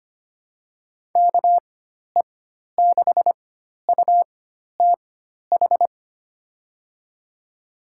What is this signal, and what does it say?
Morse code, heard as a keyed tone, about 25 wpm, spaced as for 12 wpm, 700 Hz. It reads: KE6UTH
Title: USA Call Signs - 6 Characters - 4x Character Spacing - 25wpm